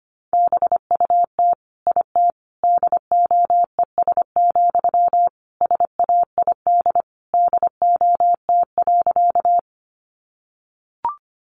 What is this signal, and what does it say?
Morse code, heard as a keyed tone, about 25 wpm, 700 Hz. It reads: BUT IT DOES, SAID DOT.
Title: Sentences from Top 1500 Words 25wpm